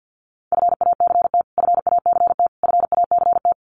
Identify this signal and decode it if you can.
Morse code, heard as a keyed tone, about 50 wpm, 700 Hz. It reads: FACT FACT FACT
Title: Top 500 Words Repeated 50wpm